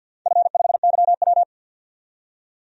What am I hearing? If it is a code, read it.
Morse code, heard as a keyed tone, about 50 wpm, 700 Hz. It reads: F5YW
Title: DXCC-100 Call Signs - 4 Characters - 50wpm